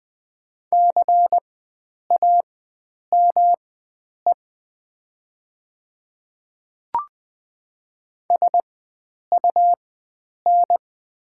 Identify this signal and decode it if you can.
Morse code, heard as a keyed tone, about 20 wpm, spaced as for 9 wpm, 700 Hz. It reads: CAME SUN
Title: 4x Character Spacing - Top 300 Words - 20wpm